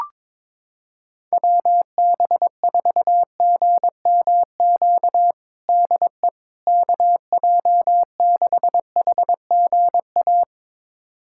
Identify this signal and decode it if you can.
Morse code, heard as a keyed tone, about 22 wpm, 700 Hz. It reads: WB4GMQ DE KJ6HGA
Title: USA Callsign DE Callsign - 6 Characters - 22wpm